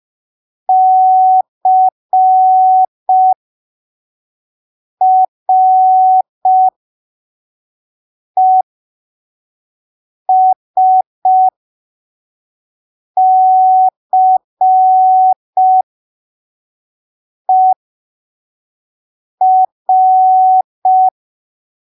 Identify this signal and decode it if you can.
Morse code, heard as a keyed tone, about 5 wpm, spaced as for 3 wpm, 750 Hz. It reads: CRESCER
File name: arquivos / texto / texto 29 05ppm